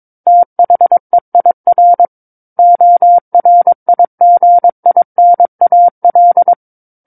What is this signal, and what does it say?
Morse code, heard as a keyed tone, about 22 wpm, 700 Hz. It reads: THEIR ORIGINAL
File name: Book One / 13-Chapter 13 - 22 WPM